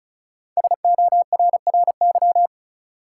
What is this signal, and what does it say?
Morse code, heard as a keyed tone, about 36 wpm, 700 Hz. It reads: SORRY
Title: Five Letter Words - 35wpm